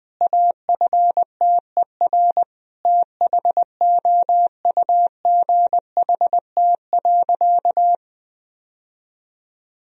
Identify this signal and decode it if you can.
Morse code, heard as a keyed tone, about 20 wpm, 700 Hz. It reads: AFTER THOUGHT.
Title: Sentences from Top 1000 Words - Review Part 2 - 20wpm